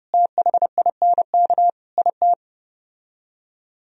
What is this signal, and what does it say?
Morse code, heard as a keyed tone, about 34 wpm, 700 Hz. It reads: THINK IT